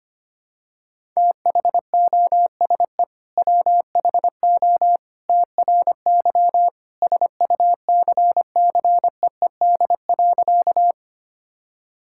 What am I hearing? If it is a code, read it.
Morse code, heard as a keyed tone, about 25 wpm, 700 Hz. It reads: THOSE WHO TRY SUCCEED.
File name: practice / 25wpm / Sentences with 2-Word Phrases - 25wpm